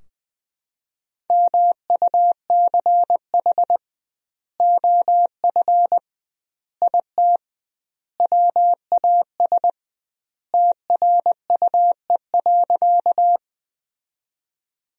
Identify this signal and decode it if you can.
Morse code, heard as a keyed tone, about 20 wpm, 700 Hz. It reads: MUCH OF IT WAS TRUE.
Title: 2x Word-Spacing - Sentences from Top 400 Words - 20wpm